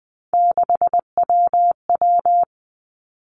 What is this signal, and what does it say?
Morse code, heard as a keyed tone, about 20 wpm, 700 Hz. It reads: 6WW
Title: Three Challenging Letter-Number Combinations 20wpm